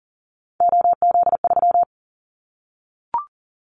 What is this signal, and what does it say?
Morse code, heard as a keyed tone, about 40 wpm, 700 Hz. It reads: OZ3
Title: Callsign Prefixes for Top 50 DXCC - 40wpm